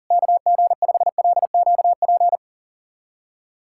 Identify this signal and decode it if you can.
Morse code, heard as a keyed tone, about 40 wpm, 700 Hz. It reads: KG5LQP
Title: USA Callsign DE Callsign - 6 Characters - 40wpm